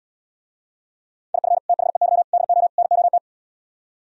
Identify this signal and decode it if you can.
Morse code, heard as a keyed tone, about 77 wpm, 700 Hz. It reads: IF THEY DO NOT